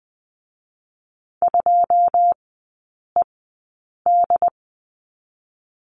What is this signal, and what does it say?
Morse code, heard as a keyed tone, about 20 wpm, 700 Hz. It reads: 2 E D
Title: 133 Lesson 22e - Question Mark - Three Characters 20wpm